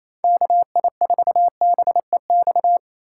Morse code, heard as a keyed tone, about 28 wpm, 690 Hz. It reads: KI4BEX
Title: USA Call Signs - 6 Characters - 4x Character Spacing - 28wpm